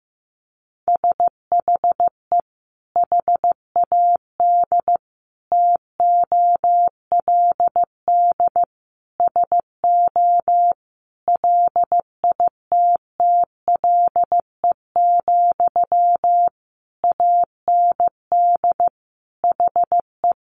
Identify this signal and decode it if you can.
Morse code, heard as a keyed tone, about 14 wpm, 700 Hz. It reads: SHE HAD TOLD SO LITTLE, AND HE
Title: Sentences from Top 1500 Words 15wpm